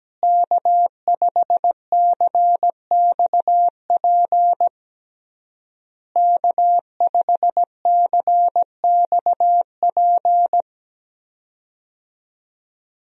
Morse code, heard as a keyed tone, about 17 wpm, 700 Hz. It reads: K5CXP K5CXP
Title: USA Call Signs - 5 Characters - Repeated 3x - 17wpm